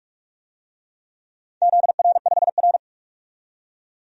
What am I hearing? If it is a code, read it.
Morse code, heard as a keyed tone, about 47 wpm, 700 Hz. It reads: ZR5R
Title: DXCC-100 Call Signs - 4 Characters - 45wpm